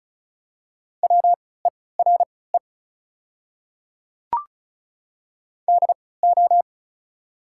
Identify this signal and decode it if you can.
Morse code, heard as a keyed tone, about 34 wpm, spaced as for 20 wpm, 700 Hz. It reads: WERE DO
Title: 3x Character Spacing - Top 300 Words - 35wpm